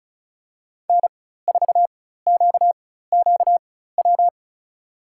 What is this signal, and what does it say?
Morse code, heard as a keyed tone, about 34 wpm, spaced as for 16 wpm, 700 Hz. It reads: N4QQW
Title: USA Call Signs - 5 Characters - 4x Character Spacing - 35wpm